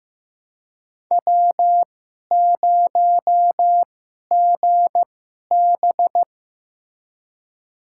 Morse code, heard as a keyed tone, about 15 wpm, spaced as for 11 wpm, 700 Hz. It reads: W0GB
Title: USA Call Signs - 4 Characters - 2x Character Spacing - 15wpm